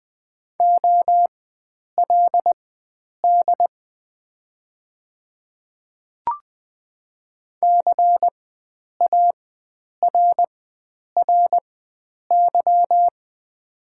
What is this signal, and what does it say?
Morse code, heard as a keyed tone, about 20 wpm, spaced as for 9 wpm, 700 Hz. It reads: OLD CARRY
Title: 4x Character Spacing - Top 300 Words - 20wpm